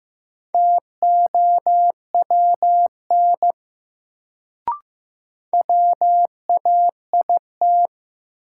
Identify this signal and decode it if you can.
Morse code, heard as a keyed tone, about 15 wpm, 700 Hz. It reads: TOWN WAIT
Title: Sentences from Top 500 Words - New Word Drill - 15wpm